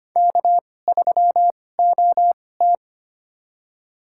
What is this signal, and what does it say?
Morse code, heard as a keyed tone, about 25 wpm, spaced as for 18 wpm, 700 Hz. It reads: K3OT
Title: USA Call Signs - 4 Characters - 2x Character Spacing - 25wpm